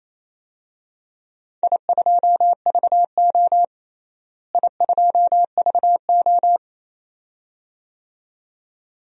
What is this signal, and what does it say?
Morse code, heard as a keyed tone, about 28 wpm, 700 Hz. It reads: I2VO I2VO
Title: DXCC-100 Call Signs - 4 Characters - Repeated 2x - 28wpm